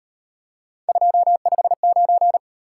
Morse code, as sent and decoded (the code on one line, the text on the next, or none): ..--- ..... ----.
259